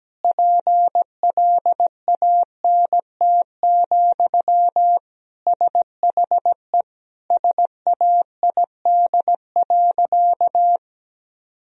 Morse code, heard as a keyed tone, about 17 wpm, 700 Hz. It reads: PLANT, SHE SAID.
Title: Sentences from Top 1000 Words - Review Part 1 - 17wpm